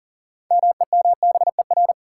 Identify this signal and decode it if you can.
Morse code, heard as a keyed tone, about 40 wpm, 700 Hz. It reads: MEMBER
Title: Sets of 3 Words - New Word Drill - 40wpm